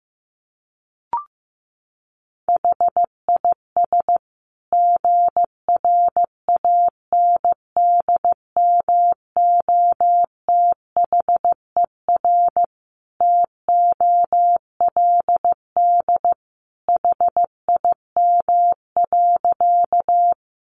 Morse code, heard as a keyed tone, about 14 wpm, 700 Hz. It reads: HIS GRANDMOTHER TOLD HIM.
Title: Sentences from Top 1200 Words 15wpm